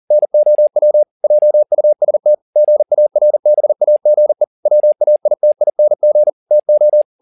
Morse code, heard as a keyed tone, about 40 wpm, 600 Hz. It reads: NOW JUST GARBAGE WAITING TO
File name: the old man and the sea 2 40wpm